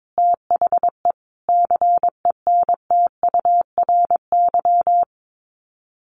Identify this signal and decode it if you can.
Morse code, heard as a keyed tone, about 22 wpm, 700 Hz. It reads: THE CENTURY
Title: Sets of 5 Words 22wpm